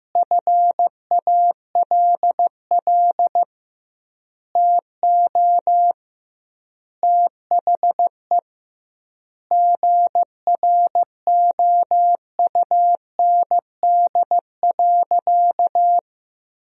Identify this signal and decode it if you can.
Morse code, heard as a keyed tone, about 15 wpm, 700 Hz. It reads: FALL TO THE GROUND.
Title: 2x Word-Spacing - Sentences from Top 500 Words - 15wpm